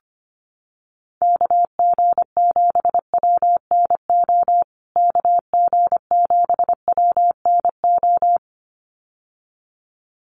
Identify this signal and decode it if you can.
Morse code, heard as a keyed tone, about 25 wpm, 700 Hz. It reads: KG7WNO KG7WNO
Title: USA Call Signs Repeated 2x 25wpm